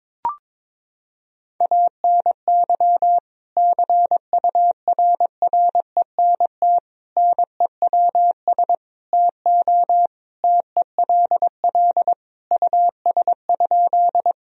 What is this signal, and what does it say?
Morse code, heard as a keyed tone, about 22 wpm, 700 Hz. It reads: ANY CURRENT NEWS TO TELL US?
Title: Sentences from Top 1000 Words - Review Part 4 - 22wpm